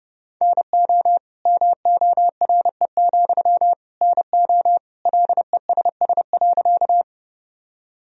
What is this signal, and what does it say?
Morse code, heard as a keyed tone, about 30 wpm, 700 Hz. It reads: NO MORE, NO LESS.
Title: Sentences from Top 1000 Words - Review Part 2 - 30wpm